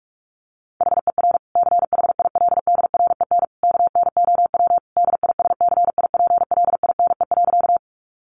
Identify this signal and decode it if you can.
Morse code, heard as a keyed tone, about 44 wpm, 700 Hz. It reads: HER CHILDREN KNOW DISCIPLINE.
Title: Sentences from Top 900 Words 45wpm